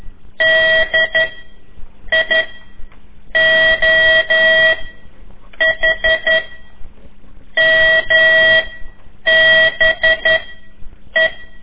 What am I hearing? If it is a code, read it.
Morse code, heard as a keyed tone, about 12 wpm, 1950 Hz. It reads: D I O H M B E
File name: ik1weg2 / CORSI CW / lez4-3